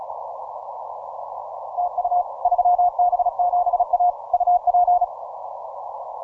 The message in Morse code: TU 3B7A UP